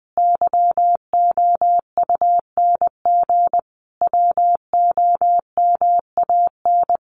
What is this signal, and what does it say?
Morse code, heard as a keyed tone, about 20 wpm, 700 Hz. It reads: YOUNG WOMAN
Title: Sentences from Top 1100 Words 20wpm